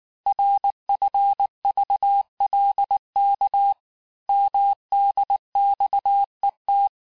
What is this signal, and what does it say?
Morse code, heard as a keyed tone, about 19 wpm, 800 Hz. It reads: RFVLK MDXET
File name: cours-du-lundi files / cours-1583--texte-chiffre-18-mpm